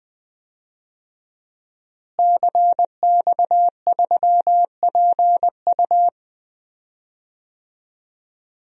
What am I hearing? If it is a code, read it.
Morse code, heard as a keyed tone, about 20 wpm, 700 Hz. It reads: CX3PU
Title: DXCC-100 Call Signs - 5 Characters - Repeated 2x - 20wpm